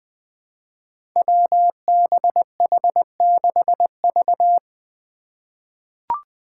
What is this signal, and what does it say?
Morse code, heard as a keyed tone, about 20 wpm, 700 Hz. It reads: WBH6V
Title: Five Challenging Letter-Number Combinations 20wpm